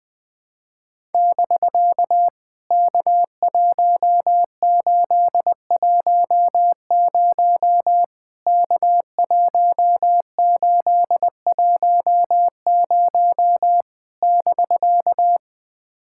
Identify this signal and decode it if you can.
Morse code, heard as a keyed tone, about 21 wpm, 700 Hz. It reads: <BK> K1810 K1810 <BK>
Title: POTA QSO Practice - 20wpm